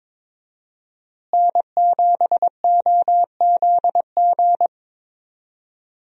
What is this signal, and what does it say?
Morse code, heard as a keyed tone, about 21 wpm, 700 Hz. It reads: N7OZG
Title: USA Call Signs - 5 Characters - Repeated 3x - 22wpm